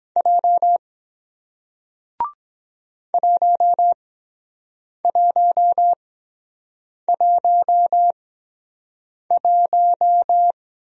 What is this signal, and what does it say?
Morse code, heard as a keyed tone, about 22 wpm, spaced as for 7 wpm, 700 Hz. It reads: J 1111